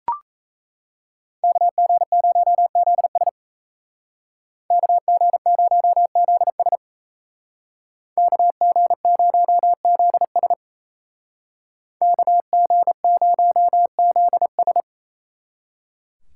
Morse code, heard as a keyed tone, about 36 wpm, 700 Hz. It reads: KG0ZS KG0ZS KG0ZS KG0ZS